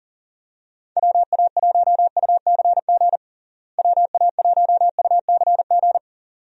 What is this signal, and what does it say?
Morse code, heard as a keyed tone, about 37 wpm, 700 Hz. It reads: WA1UCG WA1UCG
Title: USA Call Signs - 6 Characters - Repeated 2x - 40wpm